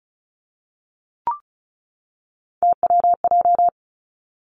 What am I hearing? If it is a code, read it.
Morse code, heard as a keyed tone, about 36 wpm, 700 Hz. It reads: TWJ
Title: Three Challenging Letter-Number Combinations 35wpm